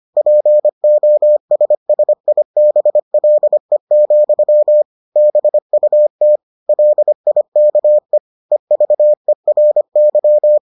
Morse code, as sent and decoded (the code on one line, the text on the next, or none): .--. --- ... ... .. -... .-.. --..-- / -... ..- - / .-.. .. -.- . / . ...- . .-. -.--
POSSIBLE, BUT LIKE EVERY